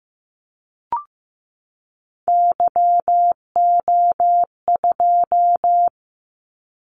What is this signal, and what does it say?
Morse code, heard as a keyed tone, about 15 wpm, 700 Hz. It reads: YO2
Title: Callsign Prefixes for Top 25 DXCC - 15wpm